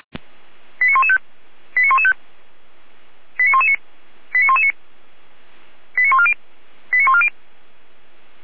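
Five-tone selective call (ZVEI), heard as quick runs of five tones, tones of about 80 ms. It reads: 89115 89115 89119 89119 89140 89140
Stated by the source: ZVEI five-tone selective call 89097 received by SDR.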